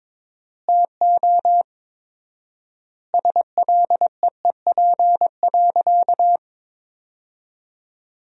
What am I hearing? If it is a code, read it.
Morse code, heard as a keyed tone, about 22 wpm, 700 Hz. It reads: TO SLEEP.
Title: 4x Word-Spacing - Sentences from Top 500 Words - 22wpm